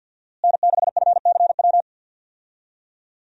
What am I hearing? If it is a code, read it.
Morse code, heard as a keyed tone, about 53 wpm, 700 Hz. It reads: N6FCW
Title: USA Call Signs - 5 Characters - Repeated 3x - 50wpm